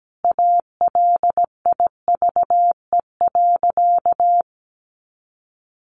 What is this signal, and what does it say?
Morse code, heard as a keyed tone, about 17 wpm, 700 Hz. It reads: ALIVE.
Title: A Sound Of Thunder 17wpm